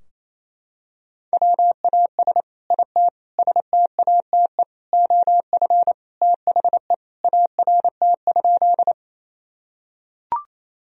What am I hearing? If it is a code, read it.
Morse code, heard as a keyed tone, about 28 wpm, 700 Hz. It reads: WAS IT STATE OF THE ART?
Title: Sentences from Top 1000 Words - Review Part 3 - 28wpm